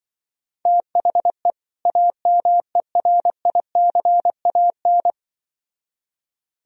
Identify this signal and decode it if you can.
Morse code, heard as a keyed tone, about 24 wpm, 700 Hz. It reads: THE AMERICAN
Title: Speed-Racing - Sets of 2 Words 20wpm